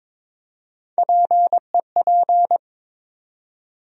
Morse code, heard as a keyed tone, about 22 wpm, 700 Hz. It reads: PEP
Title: USA Call Sign Postfixes - 3 Letters 22wpm